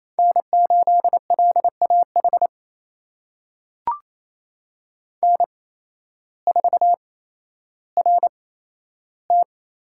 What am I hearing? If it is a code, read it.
Morse code, heard as a keyed tone, about 28 wpm, spaced as for 13 wpm, 700 Hz. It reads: N8LAH N4RT
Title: USA Call Signs - 5 Characters - 8x Character Spacing - 28wpm